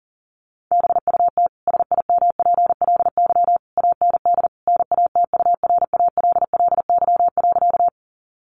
Morse code, heard as a keyed tone, about 40 wpm, 700 Hz. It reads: BUT SIMPLY AND NATURALLY.
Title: Sentences from Top 1300 Words 40wpm